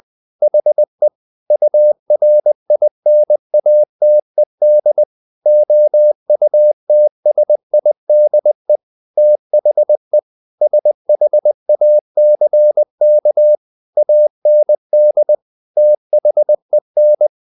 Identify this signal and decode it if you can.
Morse code, heard as a keyed tone, about 20 wpm, 600 Hz. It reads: HE URINATED OUTSIDE THE SHACK AND THEN
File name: the old man and the sea / the old man and the sea part 1 20wpm